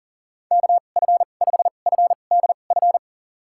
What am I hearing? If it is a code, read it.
Morse code, heard as a keyed tone, about 40 wpm, spaced as for 29 wpm, 700 Hz. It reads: KF5FDF
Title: USA Call Signs - 2x Character Spacing - 40wpm